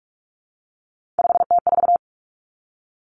Morse code, heard as a keyed tone, about 46 wpm, 700 Hz. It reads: 5T4
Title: Three Challenging Letter-Number Combinations 45wpm